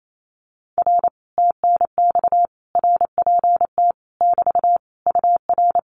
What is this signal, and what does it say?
Morse code, heard as a keyed tone, about 28 wpm, 700 Hz. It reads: R TNX RPT <BT> UR
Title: Common QSO Phrases Encore 28wpm